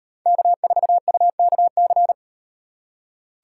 Morse code, heard as a keyed tone, about 38 wpm, 700 Hz. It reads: K4UKC